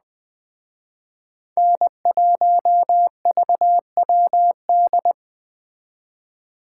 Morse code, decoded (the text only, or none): N1VWD